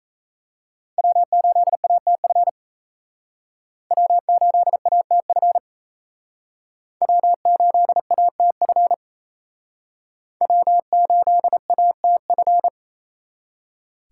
Morse code, heard as a keyed tone, about 37 wpm, 700 Hz. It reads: W8ATF W8ATF W8ATF W8ATF